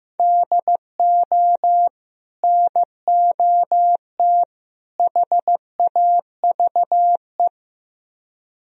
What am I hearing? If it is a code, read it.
Morse code, heard as a keyed tone, about 15 wpm, 700 Hz. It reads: DO NOT HAVE
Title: Speed-Racing - Sets of 3 Words 15wpm